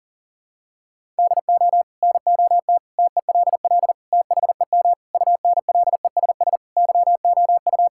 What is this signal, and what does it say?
Morse code, heard as a keyed tone, about 40 wpm, 700 Hz. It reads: DO NOT TELL THEM UNLESS YOU
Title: Sentences from Top 1000 Words - Review Part 6 - 40wpm